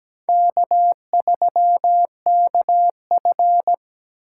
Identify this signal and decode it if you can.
Morse code, heard as a keyed tone, about 17 wpm, 700 Hz. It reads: K3KF
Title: USA Call Signs - 4 Characters - 3x Character Spacing - 17wpm